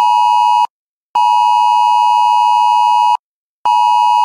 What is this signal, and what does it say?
Emergency Alert System (SAME) tones (the 853 and 960 Hz attention tone). It is heard from 1.1 to 3.2 s.